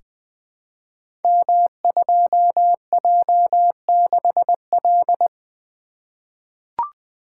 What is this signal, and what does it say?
Morse code, heard as a keyed tone, about 20 wpm, 700 Hz. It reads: M2J6L
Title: Code Groups - 5 Letter:Numbers - 20wpm